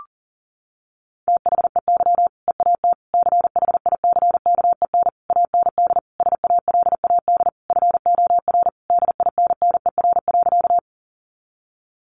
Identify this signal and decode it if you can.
Morse code, heard as a keyed tone, about 40 wpm, 700 Hz. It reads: THEY EAT CHICKEN AND SALAD FOR DINNER.
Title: Easy Phrases 40wpm